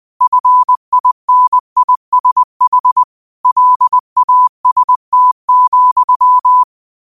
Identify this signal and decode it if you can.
Morse code, heard as a keyed tone, about 20 wpm, 1000 Hz. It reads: FINISH LAST,
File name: qotdfiles / 20wpm